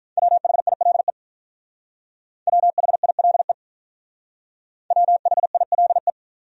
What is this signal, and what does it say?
Morse code, heard as a keyed tone, about 47 wpm, 700 Hz. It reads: WHILE WHILE WHILE